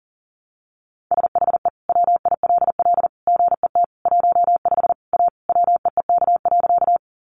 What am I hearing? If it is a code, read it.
Morse code, heard as a keyed tone, about 40 wpm, 700 Hz. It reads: SHE WILL GET 15 A WEEK.
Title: Easy Phrases 40wpm